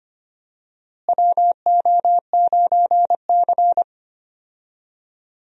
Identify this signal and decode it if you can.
Morse code, heard as a keyed tone, about 25 wpm, 700 Hz. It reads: WO9C
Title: USA Call Signs - 4 Characters - Repeated 3x - 25wpm